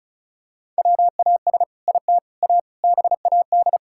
Morse code, heard as a keyed tone, about 36 wpm, 700 Hz. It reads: WAS IT A BAD
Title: Sentences from Top 1000 Words - Review Part 5 - 35wpm